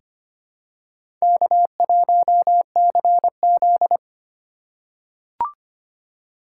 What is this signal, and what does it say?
Morse code, heard as a keyed tone, about 25 wpm, 700 Hz. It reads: K1CZ
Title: USA Call Signs - 4 Characters - 2x Character Spacing - 25wpm